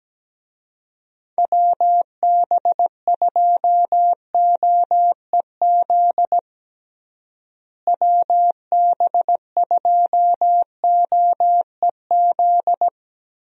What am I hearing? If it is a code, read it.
Morse code, heard as a keyed tone, about 17 wpm, 700 Hz. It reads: WB2OEZ WB2OEZ
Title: USA Call Signs - 6 Characters - Repeated 2x - 17wpm